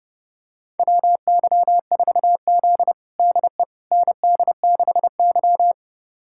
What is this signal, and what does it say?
Morse code, heard as a keyed tone, about 30 wpm, 700 Hz. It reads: WY4Z DE ND6Y
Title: USA Callsign DE Callsign - 4 Characters - 30wpm